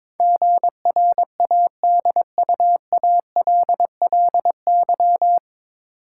Morse code, heard as a keyed tone, about 22 wpm, 700 Hz. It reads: GRADUALLY
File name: 9 Letter Words - First 3 Letters Given - 22wpm